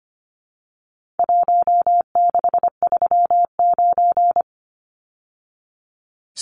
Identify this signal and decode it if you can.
Morse code, heard as a keyed tone, about 25 wpm, 700 Hz. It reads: 1639
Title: Numbers - Contest Style - 4-Digit 25wpm